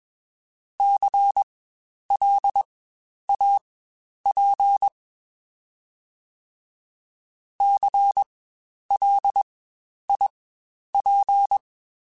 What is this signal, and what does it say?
Morse code, heard as a keyed tone, about 20 wpm, spaced as for 10 wpm, 800 Hz. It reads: CLAP CLIP
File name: Spaced Alphas / Spaced Alphas mp3s / Cs sp @20